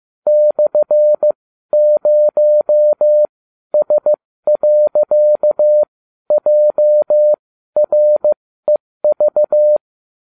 Morse code, heard as a keyed tone, about 15 wpm, spaced as for 12 wpm, 600 Hz. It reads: /0S.JREV